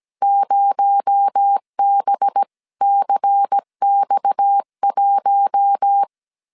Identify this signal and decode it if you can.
Morse code, heard as a keyed tone, about 17 wpm, 800 Hz. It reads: 0BCX1
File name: 16wpm